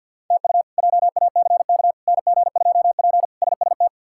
Morse code, heard as a keyed tone, about 51 wpm, 700 Hz. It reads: TU JACK NG2P SST
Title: K1USN SST QSO Practice - 50wpm